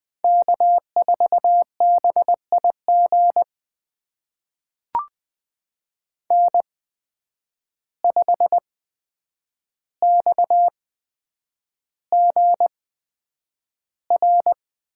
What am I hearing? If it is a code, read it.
Morse code, heard as a keyed tone, about 20 wpm, spaced as for 9 wpm, 700 Hz. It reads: K4BIG N5XGR
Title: USA Call Signs - 5 Characters - 8x Character Spacing - 20wpm